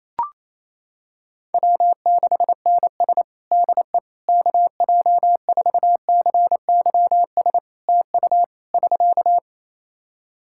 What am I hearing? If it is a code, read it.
Morse code, heard as a keyed tone, about 28 wpm, 700 Hz. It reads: W6NS DE KJ4CYS TU <SK>
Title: Practice QSO - Exchange 4 - Ending - 28wpm